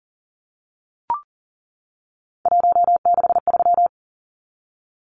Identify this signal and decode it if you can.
Morse code, heard as a keyed tone, about 40 wpm, 700 Hz. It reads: 163
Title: Numbers - 3-Digit 40wpm